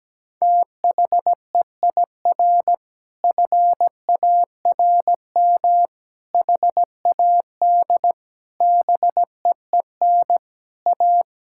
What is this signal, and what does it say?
Morse code, heard as a keyed tone, about 17 wpm, 700 Hz. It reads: THEIR FARM HAD BEEN A
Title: Sentences from Top 1000 Words - Review Part 6 - 17wpm